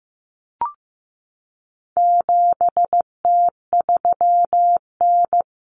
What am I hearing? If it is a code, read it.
Morse code, heard as a keyed tone, about 15 wpm, 700 Hz. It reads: 7T3N
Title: Common Cut Numbers - 4-Digit 15wpm